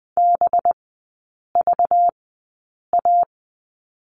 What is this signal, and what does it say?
Morse code, heard as a keyed tone, about 20 wpm, spaced as for 8 wpm, 700 Hz. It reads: BVA
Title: 191 Lesson 31e - Letter v - Three Characters 20wpm